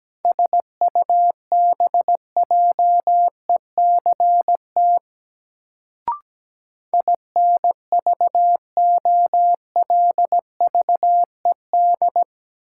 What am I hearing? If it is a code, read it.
Morse code, heard as a keyed tone, about 17 wpm, 700 Hz. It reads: SUBJECT INVOLVED
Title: Sentences from Top 800 Words - New Word Drill - 17wpm